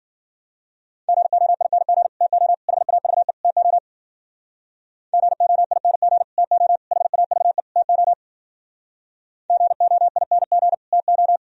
Morse code, heard as a keyed tone, about 51 wpm, 700 Hz. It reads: GOING TO HAVE TO GOING TO HAVE TO GOING TO